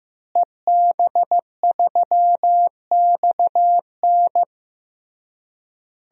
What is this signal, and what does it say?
Morse code, heard as a keyed tone, about 15 wpm, 700 Hz. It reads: EB3XN
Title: DXCC-100 Call Signs - 5 Characters - Repeated 3x - 15wpm